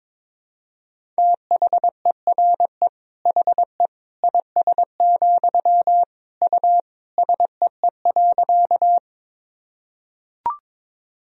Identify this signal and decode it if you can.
Morse code, heard as a keyed tone, about 22 wpm, 700 Hz. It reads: THERE HE IS, U SEE.
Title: Sentences from Top 100 Words and Abbreviations 22wpm